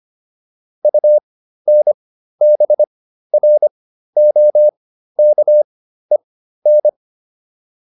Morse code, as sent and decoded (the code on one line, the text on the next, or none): ..- -. -... .-. --- -.- . -.
UNBROKEN